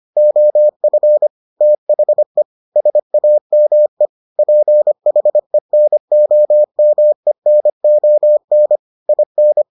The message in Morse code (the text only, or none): OF THE SAME PHENOMENON IN